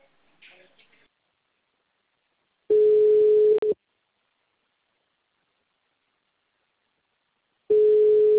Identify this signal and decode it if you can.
Ringback tone (Europe, 425 Hz).